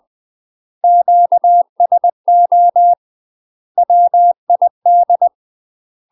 Morse code, heard as a keyed tone, about 20 wpm, 700 Hz. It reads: QSO WID
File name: UK MASTER QSOs TRAINING / 20WPM / UK MASTER QSOs 0036